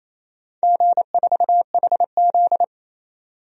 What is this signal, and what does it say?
Morse code, heard as a keyed tone, about 28 wpm, 700 Hz. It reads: G4HZ
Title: DXCC-100 Call Signs - 4 Characters - 28wpm